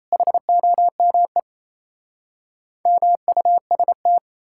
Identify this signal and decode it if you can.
Morse code, heard as a keyed tone, about 37 wpm, 700 Hz. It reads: HOME MUST